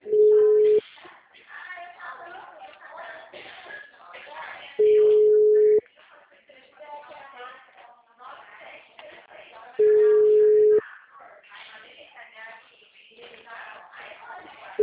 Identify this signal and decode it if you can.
Ringback tone (Europe, 425 Hz), with speech.
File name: out- -2010- - - .89973